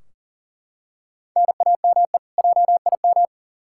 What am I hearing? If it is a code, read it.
Morse code, heard as a keyed tone, about 40 wpm, 700 Hz. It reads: NAME JIM
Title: K1USN SST QSO Practice 40wpm